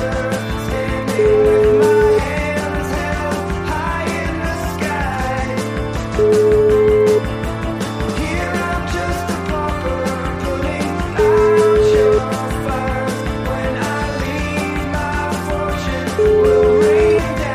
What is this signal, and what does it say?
Ringback tone (Europe, 425 Hz), with noise.